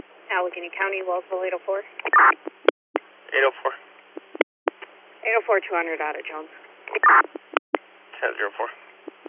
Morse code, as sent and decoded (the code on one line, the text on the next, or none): none